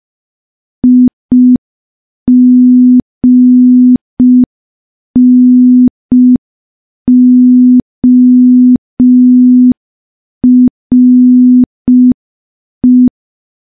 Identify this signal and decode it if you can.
Morse code, heard as a keyed tone, about 5 wpm, 250 Hz. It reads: IGNORE
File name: morse / morse-training-05WPM